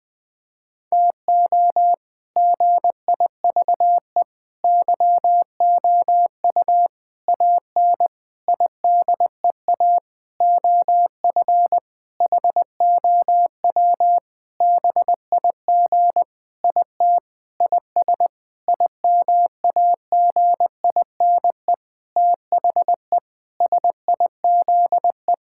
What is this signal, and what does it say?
Morse code, heard as a keyed tone, about 20 wpm, 700 Hz. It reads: TO GIVE YOU AN IDEA OF HOW BIG IT IS IMAGINE THE SIZE
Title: Sentences with 7-Word Phrases - 20wpm